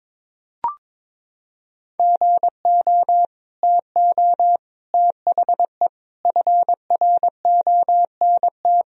Morse code, heard as a keyed tone, about 22 wpm, 700 Hz. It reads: GO TO THE FRONT
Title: Sentences from Top 1000 Words - Review Part 2 - 22wpm